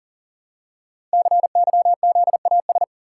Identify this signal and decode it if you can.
Morse code, heard as a keyed tone, about 39 wpm, 700 Hz. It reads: CYZAS
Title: Code Groups - 5 Letters - 40wpm